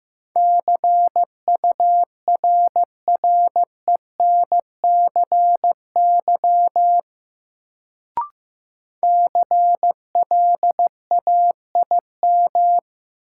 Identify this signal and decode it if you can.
Morse code, heard as a keyed tone, about 15 wpm, 700 Hz. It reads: CURRENCY CLAIM
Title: Sentences from Top 1500 Words - New Word Drill Alt - 15wpm